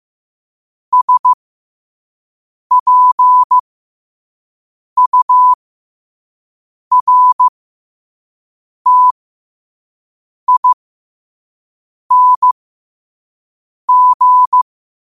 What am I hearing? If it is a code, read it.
Morse code, heard as a keyed tone, about 15 wpm, spaced as for 5 wpm, 1000 Hz. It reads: SPURTING
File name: qotdfiles / 5wpm